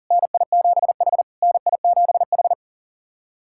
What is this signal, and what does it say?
Morse code, heard as a keyed tone, about 40 wpm, 690 Hz. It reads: NI7H NI7H
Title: SOTA QSO Practice 40wpm